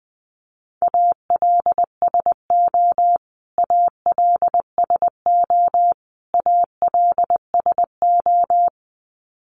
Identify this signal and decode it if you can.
Morse code, heard as a keyed tone, about 20 wpm, 700 Hz. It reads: ALSO ALSO ALSO
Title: Top 500 Words Repeated 20wpm